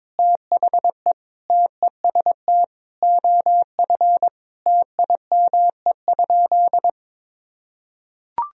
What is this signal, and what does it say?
Morse code, heard as a keyed tone, about 22 wpm, 700 Hz. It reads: THE TEST OF TIME?
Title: Sentences from Top 1000 Words - Review Part 3 - 22wpm